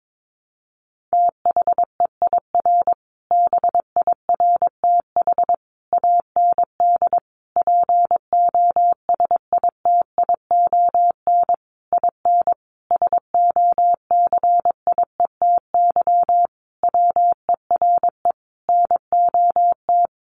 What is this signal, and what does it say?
Morse code, heard as a keyed tone, about 22 wpm, 700 Hz. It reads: THEIR BIRTH AND POSITION IN SOCIETY WERE NOT
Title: Sentences from Top 900 Words 22wpm